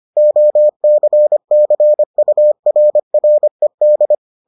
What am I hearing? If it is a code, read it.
Morse code, heard as a keyed tone, about 25 wpm, 600 Hz. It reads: OCCURRED